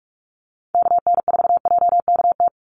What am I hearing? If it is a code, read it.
Morse code, heard as a keyed tone, about 44 wpm, 700 Hz. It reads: KN4JKT
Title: USA Call Signs 45wpm